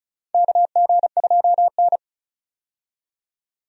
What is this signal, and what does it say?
Morse code, heard as a keyed tone, about 34 wpm, 700 Hz. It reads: KG2N
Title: USA Call Signs - 4 Characters - 35wpm